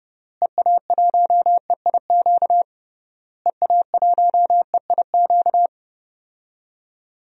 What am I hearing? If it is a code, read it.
Morse code, heard as a keyed tone, about 30 wpm, 700 Hz. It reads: EA1EIQ EA1EIQ
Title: DXCC-100 Call Signs - 6 Characters - Repeated 3x - 30wpm